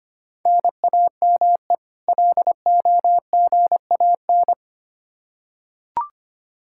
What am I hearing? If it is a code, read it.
Morse code, heard as a keyed tone, about 25 wpm, 700 Hz. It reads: NAME LOGAN
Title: Practice QSO - Exchange 1 - Beginning - 25wpm